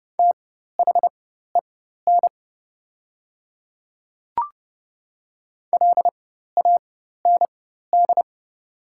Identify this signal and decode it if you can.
Morse code, heard as a keyed tone, about 30 wpm, spaced as for 14 wpm, 700 Hz. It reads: THEN LAND
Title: 4x Character Spacing - Top 300 Words - 30wpm